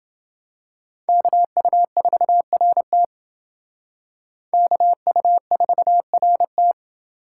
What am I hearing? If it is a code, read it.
Morse code, heard as a keyed tone, about 28 wpm, 700 Hz. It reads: KU4RT KU4RT